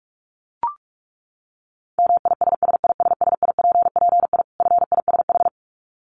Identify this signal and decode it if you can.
Morse code, heard as a keyed tone, about 44 wpm, 700 Hz. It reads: MISSISSIPPI FISH
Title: Dit Challenge 45wpm